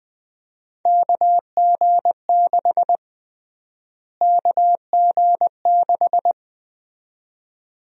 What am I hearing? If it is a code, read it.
Morse code, heard as a keyed tone, about 20 wpm, 700 Hz. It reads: KG6 KG6
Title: USA Call Sign Prefixes - Repeated 2x - 20wpm